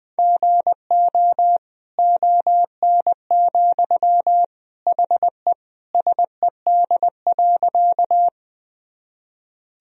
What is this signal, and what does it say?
Morse code, heard as a keyed tone, about 20 wpm, 700 Hz. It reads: GO ON, HE SED.
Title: Sentences from Top 100 Words and Abbreviations 20wpm